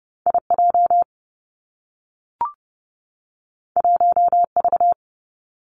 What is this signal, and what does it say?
Morse code, heard as a keyed tone, about 30 wpm, 700 Hz. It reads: I J 1 V
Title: Two Challenging Letter-Number Combinations 30wpm